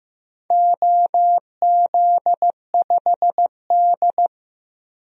Morse code, heard as a keyed tone, about 15 wpm, 700 Hz. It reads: OZ5D
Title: DXCC-100 Call Signs - 4 Characters - 15wpm